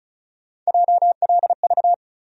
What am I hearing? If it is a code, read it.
Morse code, heard as a keyed tone, about 36 wpm, 700 Hz. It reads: JLV
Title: USA Call Sign Postfixes - 3 Letters 35wpm